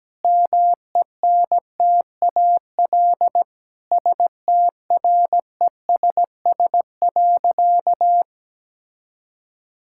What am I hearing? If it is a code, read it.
Morse code, heard as a keyed tone, about 17 wpm, 700 Hz. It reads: MENTAL STRESS.
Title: Sentences from Top 1000 Words - Review Part 6 - 17wpm